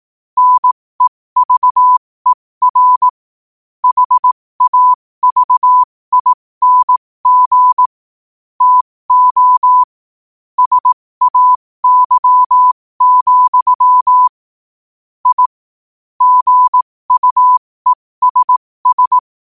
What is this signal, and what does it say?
Morse code, heard as a keyed tone, about 18 wpm, spaced as for 15 wpm, 1000 Hz. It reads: NEVER HAVING TO SAY, I GUESS
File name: qotdfiles / 15wpm